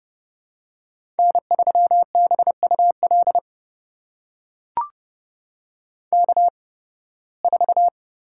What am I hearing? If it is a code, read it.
Morse code, heard as a keyed tone, about 30 wpm, spaced as for 20 wpm, 700 Hz. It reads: N3BUL K4
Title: USA Call Signs - 5 Characters - 8x Character Spacing - 30wpm